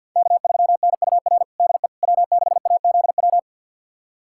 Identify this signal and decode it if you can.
Morse code, heard as a keyed tone, about 46 wpm, 700 Hz. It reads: K3NUR DE W6AZW
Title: USA Callsign DE Callsign - 5 Characters - 50wpm